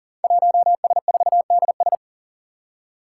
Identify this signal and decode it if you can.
Morse code, heard as a keyed tone, about 40 wpm, 700 Hz. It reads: 1S4DS
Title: Five Challenging Letter-Number Combinations 40wpm